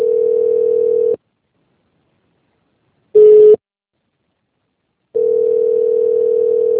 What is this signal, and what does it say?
Ringback tone (North America), 440 + 480 Hz.